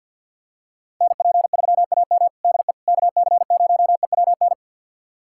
Morse code, heard as a keyed tone, about 51 wpm, 700 Hz. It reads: NP3AM DE KC0EWN
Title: USA Callsign DE Callsign 50wpm